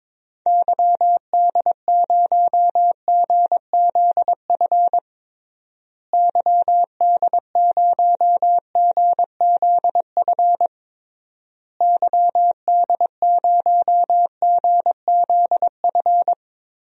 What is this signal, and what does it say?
Morse code, heard as a keyed tone, about 22 wpm, 700 Hz. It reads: YD0GZF YD0GZF YD0GZF
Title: DXCC-100 Call Signs - 6 Characters - Repeated 3x - 22wpm